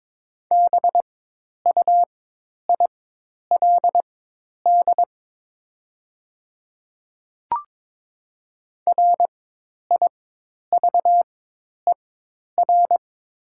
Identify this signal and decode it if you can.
Morse code, heard as a keyed tone, about 22 wpm, spaced as for 10 wpm, 700 Hz. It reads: BUILD RIVER
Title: 4x Character Spacing - Top 300 Words - 22wpm